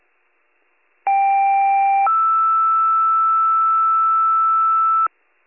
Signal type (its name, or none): two-tone page